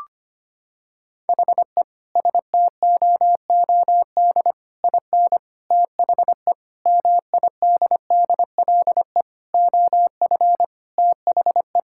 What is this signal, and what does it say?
Morse code, heard as a keyed tone, about 25 wpm, 700 Hz. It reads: HE STOOD IN THE MIDDLE OF THE
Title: Sentences from Top 1000 Words - Review Part 6 - 25wpm